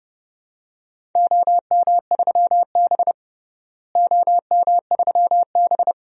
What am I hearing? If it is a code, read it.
Morse code, heard as a keyed tone, about 31 wpm, 700 Hz. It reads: OM3B OM3B
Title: DXCC-100 Call Signs - 4 Characters - Repeated 2x - 30wpm